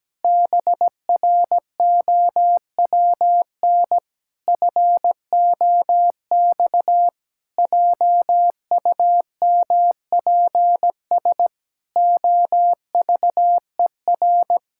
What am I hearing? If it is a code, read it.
Morse code, heard as a keyed tone, about 17 wpm, 700 Hz. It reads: BROWN FOX JUMPS OVER
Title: Pangram Challenge v2 - 17wpm